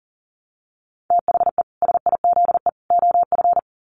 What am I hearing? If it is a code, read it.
Morse code, heard as a keyed tone, about 37 wpm, 700 Hz. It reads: THE SIZE OF
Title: Sets of 3 Words 40wpm